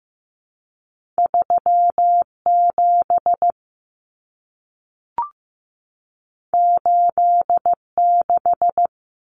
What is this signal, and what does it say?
Morse code, heard as a keyed tone, about 15 wpm, 700 Hz. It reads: 37 86
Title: Common Cut Numbers - 2-Digit 15wpm